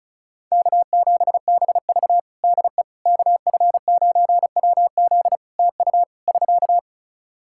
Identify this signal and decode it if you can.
Morse code, heard as a keyed tone, about 34 wpm, 700 Hz. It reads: K7BV DE KF9WZ TU <SK>
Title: K1USN SST QSO Practice 35wpm